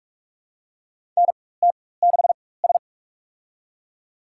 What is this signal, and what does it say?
Morse code, heard as a keyed tone, about 46 wpm, spaced as for 21 wpm, 700 Hz. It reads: NT6S
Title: USA Call Signs - 4 Characters - 4x Character Spacing - 45wpm